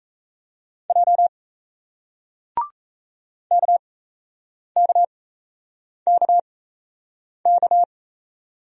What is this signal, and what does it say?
Morse code, heard as a keyed tone, about 38 wpm, spaced as for 8 wpm, 700 Hz. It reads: J KKKK